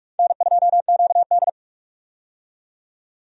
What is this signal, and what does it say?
Morse code, heard as a keyed tone, about 46 wpm, 700 Hz. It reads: N2QD
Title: USA Call Signs - 4 Characters - 45wpm